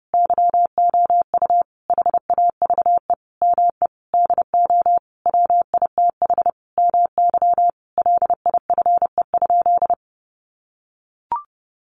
Morse code, heard as a keyed tone, about 30 wpm, 700 Hz. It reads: YOU HAVE ME DO WITH MY LIFE?
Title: Sentences from Top 300 Words 30wpm